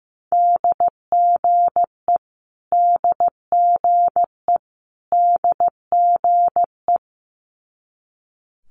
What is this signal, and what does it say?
Morse code, heard as a keyed tone, about 15 wpm, 700 Hz. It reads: DGE DGE DGE
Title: Multi-letter Phonemes Repeated 3x 15wpm